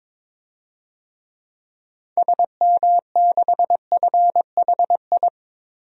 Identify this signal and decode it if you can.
Morse code, heard as a keyed tone, about 23 wpm, 700 Hz. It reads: SM6FHI
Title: DXCC-100 Call Signs - 6 Characters - 22wpm